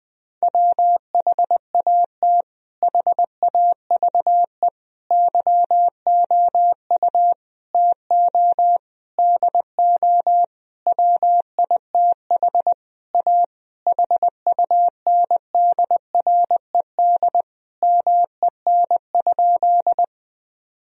Morse code, heard as a keyed tone, about 20 wpm, 700 Hz. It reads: WHAT HAVE YOU TO DO WITH A HUNDRED MEN?
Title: Sentences from Top 1000 Words - Review Part 2 - 20wpm